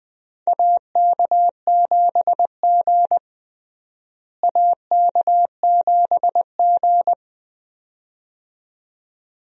Morse code, heard as a keyed tone, about 20 wpm, 700 Hz. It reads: AK7G AK7G
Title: USA Call Signs - 4 Characters - Repeated 2x - 20wpm